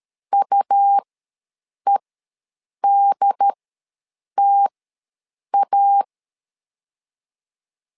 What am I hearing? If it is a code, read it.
Morse code, heard as a keyed tone, about 13 wpm, spaced as for 7 wpm, 800 Hz. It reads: UEDTA
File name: LEZIONI / 2 UHDN / 2 UHD ws 12wpm 1722